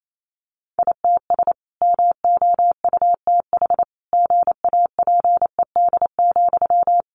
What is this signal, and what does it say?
Morse code, heard as a keyed tone, about 28 wpm, 700 Hz. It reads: ITS MOUTH GAPED,
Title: A Sound Of Thunder 28wpm